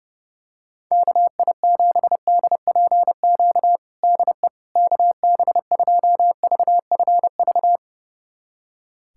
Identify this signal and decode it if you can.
Morse code, heard as a keyed tone, about 32 wpm, 700 Hz. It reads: KI7DPQ DE KB2VFV
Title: USA Callsign DE Callsign - 6 Characters - 30wpm